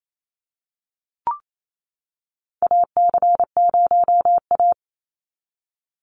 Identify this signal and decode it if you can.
Morse code, heard as a keyed tone, about 28 wpm, 700 Hz. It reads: AC0A
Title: NA SOTA Chaser Callsigns 28wpm